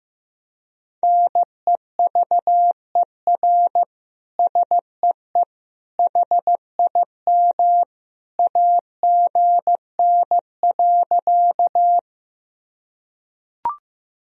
Morse code, heard as a keyed tone, about 15 wpm, 700 Hz. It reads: NEVER SEE HIM AGN.
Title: Sentences from Top 300 Words and Abbreviations 15wpm